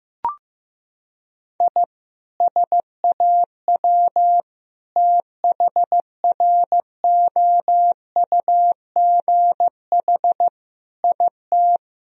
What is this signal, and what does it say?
Morse code, heard as a keyed tone, about 15 wpm, 700 Hz. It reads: I SAW THROUGH IT
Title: Sentences from Top 1000 Words - Review Part 1 - 15wpm